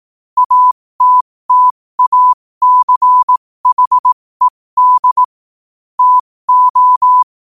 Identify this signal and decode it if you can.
Morse code, heard as a keyed tone, about 18 wpm, spaced as for 15 wpm, 1000 Hz. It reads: ATTACHED TO
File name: qotdfiles / 15wpm